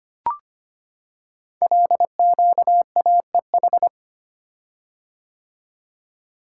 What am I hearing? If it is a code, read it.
Morse code, heard as a keyed tone, about 25 wpm, 700 Hz. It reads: LQAEH
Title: Code Groups - 5 Letters - 25wpm